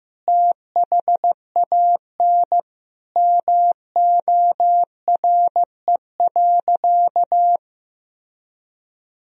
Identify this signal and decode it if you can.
Morse code, heard as a keyed tone, about 15 wpm, 700 Hz. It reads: THAN MORE.
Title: Sentences from Top 1000 Words - Review Part 4 - 15wpm